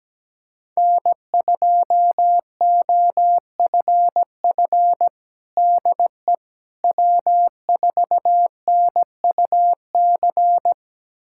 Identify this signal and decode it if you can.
Morse code, heard as a keyed tone, about 17 wpm, 700 Hz. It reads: N2OFF DE W4NUC
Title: USA Callsign DE Callsign - 5 Characters - 17wpm